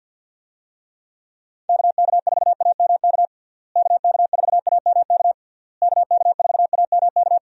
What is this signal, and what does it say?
Morse code, heard as a keyed tone, about 52 wpm, 700 Hz. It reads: KK4AMK KK4AMK KK4AMK
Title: USA Call Signs - 6 Characters - Repeated 3x - 50wpm